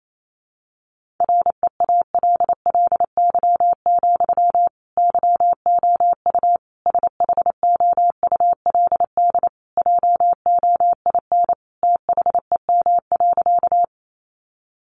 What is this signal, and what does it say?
Morse code, heard as a keyed tone, about 28 wpm, 700 Hz. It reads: REALLY, YOU SHOULD JOIN THEM.
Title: Sentences from Top 1400 Words 28wpm